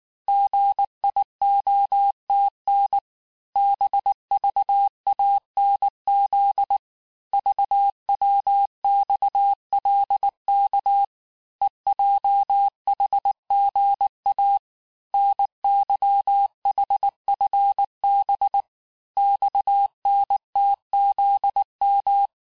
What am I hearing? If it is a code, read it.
Morse code, heard as a keyed tone, about 19 wpm, 800 Hz. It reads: GIOTN BVANZ VWXLK EJHGA NYHFB XNTZM
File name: cours-du-lundi files / cours-1589-texte-chiffre-vitesse-18mpm